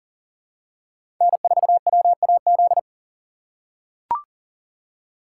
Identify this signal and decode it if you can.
Morse code, heard as a keyed tone, about 40 wpm, 700 Hz. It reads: N4WAZ
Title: USA Call Signs - 5 Characters - 3x Character Spacing - 40wpm